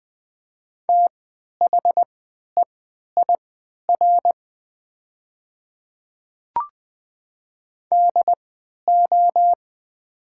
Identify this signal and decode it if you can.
Morse code, heard as a keyed tone, about 20 wpm, spaced as for 11 wpm, 700 Hz. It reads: THEIR DO
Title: 3x Character Spacing - Top 100 Words - 20wpm